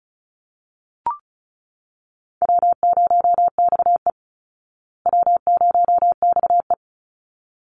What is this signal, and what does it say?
Morse code, heard as a keyed tone, about 34 wpm, 700 Hz. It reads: W0XE W0XE
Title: Easy Call Signs Repeated 2x - 35wpm